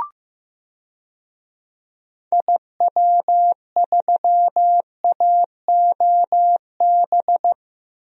Morse code, heard as a keyed tone, about 15 wpm, 700 Hz. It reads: IW3AOB
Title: DXCC-100 Call Signs - 6 Characters - Repeated 2x - 15wpm